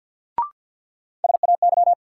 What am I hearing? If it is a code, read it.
Morse code, heard as a keyed tone, about 50 wpm, 700 Hz. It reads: SAY
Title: Sets of 2 Words - New Word Drill - 50wpm